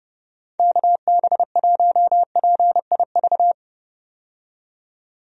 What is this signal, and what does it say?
Morse code, heard as a keyed tone, about 32 wpm, 700 Hz. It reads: KB1PIV
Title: USA Call Signs - 6 Characters - 30wpm